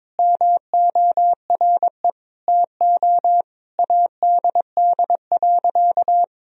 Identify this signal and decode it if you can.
Morse code, heard as a keyed tone, about 21 wpm, 700 Hz. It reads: MORE TO ADD.
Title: Sentences from Top 1000 Words - Review Part 1 - 22wpm